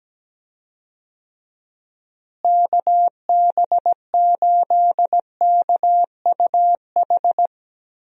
Morse code, heard as a keyed tone, about 17 wpm, 700 Hz. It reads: KB8KUH
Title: USA Call Signs - 6 Characters - Repeated 2x - 17wpm